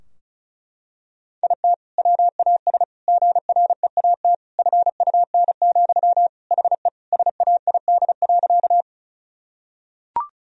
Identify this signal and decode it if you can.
Morse code, heard as a keyed tone, about 36 wpm, 700 Hz. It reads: IT WAS GREAT FUN, HE SAID.
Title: Sentences from Top 1000 Words - Review Part 4 - 35wpm